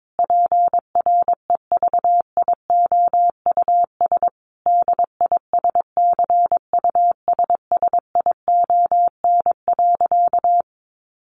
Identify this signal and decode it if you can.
Morse code, heard as a keyed tone, about 22 wpm, 700 Hz. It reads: PREVIOUS DISCUSSION.
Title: Sentences from Top 1500 Words 22wpm